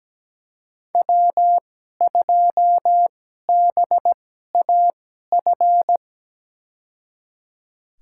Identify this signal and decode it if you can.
Morse code, heard as a keyed tone, about 17 wpm, spaced as for 12 wpm, 700 Hz. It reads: W2BAF
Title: USA Call Signs - 2x Character Spacing - 17wpm